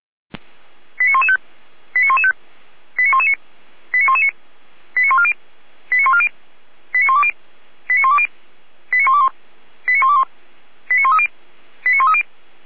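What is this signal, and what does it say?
Five-tone selective call (ZVEI), heard as quick runs of five tones, tones of about 70 ms. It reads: 89115 89115 89119 89119 89140 89140 89120 89120 89121 89121 89130 89130
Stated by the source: ZVEI five-tone selective call 89110 received with an SDR.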